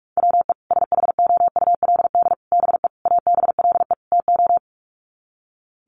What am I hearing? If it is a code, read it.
Morse code, heard as a keyed tone, about 44 wpm, 700 Hz. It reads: WE SHOULD BE ABLE TO
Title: Sets of 5 Words Encore2 45wpm